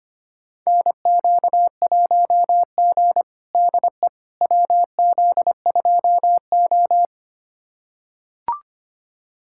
Morse code, heard as a keyed tone, about 25 wpm, 700 Hz. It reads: NQ1G DE WZ2O
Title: USA Callsign DE Callsign - 4 Characters - 25wpm